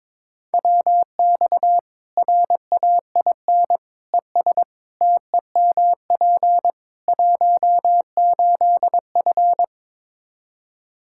Morse code, heard as a keyed tone, about 22 wpm, 700 Hz. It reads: WX RAIN ES TEMP 18F
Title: K1USN SST QSO Practice 22wpm